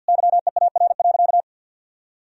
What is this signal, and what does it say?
Morse code, heard as a keyed tone, about 49 wpm, 700 Hz. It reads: YEAR.
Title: Sentences from Top 1000 Words - Review Part 1 - 50wpm